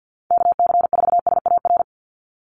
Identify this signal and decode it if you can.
Morse code, heard as a keyed tone, about 50 wpm, 700 Hz. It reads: KC4SAR
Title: USA Call Signs 50wpm